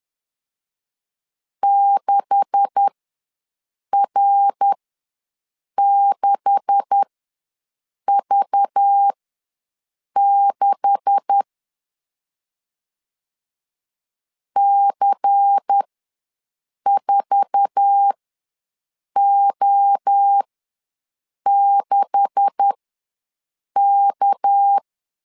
Morse code, heard as a keyed tone, about 10 wpm, spaced as for 6 wpm, 800 Hz. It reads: 6R6V6 C4O6K
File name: LEZIONI / 9 286 ws 10wpm 4032